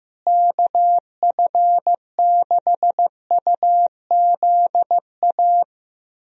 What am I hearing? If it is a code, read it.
Morse code, heard as a keyed tone, about 15 wpm, 700 Hz. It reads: KF6UZA
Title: USA Call Signs - 6 Characters - Repeated 3x - 15wpm